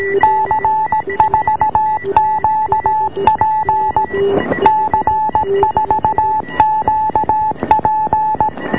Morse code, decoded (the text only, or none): C4QP C4QP